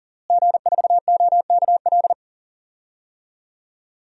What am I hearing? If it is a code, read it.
Morse code, heard as a keyed tone, about 40 wpm, 700 Hz. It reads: G4OKL
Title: DXCC-100 Call Signs - 5 Characters - Repeated 3x - 40wpm